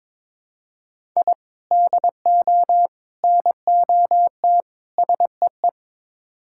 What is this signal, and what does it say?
Morse code, heard as a keyed tone, about 21 wpm, 700 Hz. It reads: I DO NOT SEE
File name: practice / 22wpm / Build A Phrase - 5-Word Phrases - 22wpm